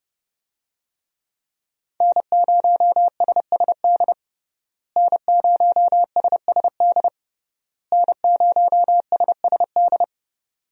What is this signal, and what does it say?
Morse code, heard as a keyed tone, about 29 wpm, 700 Hz. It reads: N0SSD N0SSD N0SSD
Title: USA Call Signs - 5 Characters - Repeated 3x - 30wpm